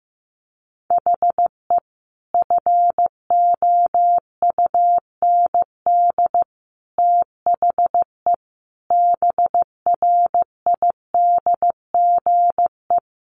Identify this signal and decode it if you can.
Morse code, heard as a keyed tone, about 15 wpm, 700 Hz. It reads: HE FOUND THE BRIDGE
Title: Sentences from Top 1200 Words 15wpm